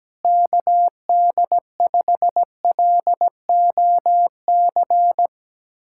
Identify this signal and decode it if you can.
Morse code, heard as a keyed tone, about 17 wpm, 700 Hz. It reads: KD5LOC
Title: USA Callsign DE Callsign 17wpm